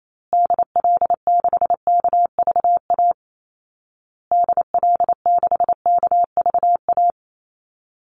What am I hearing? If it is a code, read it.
Morse code, heard as a keyed tone, about 28 wpm, 700 Hz. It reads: DL6KVA DL6KVA
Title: Call Signs Repeated 2x - 28wpm